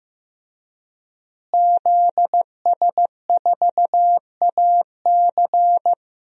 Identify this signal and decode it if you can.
Morse code, heard as a keyed tone, about 15 wpm, 700 Hz. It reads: ZS4AC
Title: DXCC-100 Call Signs - 5 Characters - Repeated 2x - 15wpm